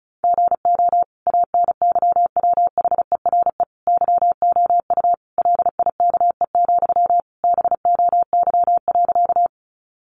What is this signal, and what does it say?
Morse code, heard as a keyed tone, about 34 wpm, 700 Hz. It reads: GO ANYWHERE YOU LIKE, BOY.
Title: Sentences from Top 1400 Words 35wpm